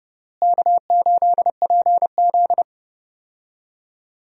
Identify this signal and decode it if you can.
Morse code, heard as a keyed tone, about 32 wpm, 700 Hz. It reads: K8PZ
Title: USA Call Signs - 4 Characters - Repeated 3x - 30wpm